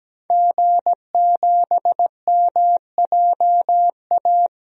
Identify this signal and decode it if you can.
Morse code, heard as a keyed tone, about 17 wpm, 700 Hz. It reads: G7MJA
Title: DXCC-100 Call Signs - 5 Characters - Repeated 2x - 17wpm